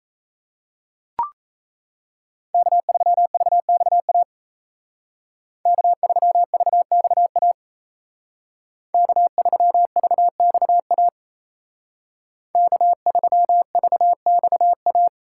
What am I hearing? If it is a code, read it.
Morse code, heard as a keyed tone, about 36 wpm, 700 Hz. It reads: K3VXA K3VXA K3VXA K3VXA